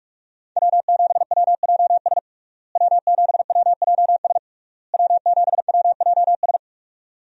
Morse code, heard as a keyed tone, about 46 wpm, 700 Hz. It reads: W7WJS W7WJS W7WJS
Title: USA Call Signs - 5 Characters - Repeated 3x - 45wpm